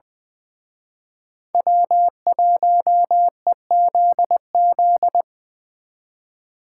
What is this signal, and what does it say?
Morse code, heard as a keyed tone, about 24 wpm, 700 Hz. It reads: W1EZZ
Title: Speed Racing - USA 1x3 Call Signs 20wpm